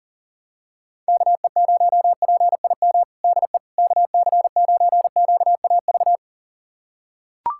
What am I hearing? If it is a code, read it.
Morse code, heard as a keyed tone, about 40 wpm, 700 Hz. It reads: KE0PIM DE KC9QAV
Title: USA Callsign DE Callsign - 6 Characters - 40wpm